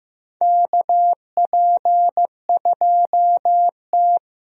Morse code, heard as a keyed tone, about 15 wpm, 700 Hz. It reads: KP2T
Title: DXCC-100 Call Signs - 4 Characters - Repeated 3x - 15wpm